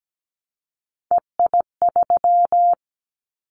Morse code, heard as a keyed tone, about 17 wpm, 700 Hz. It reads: EI3
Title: Callsign Prefixes for Top 50 DXCC - 17wpm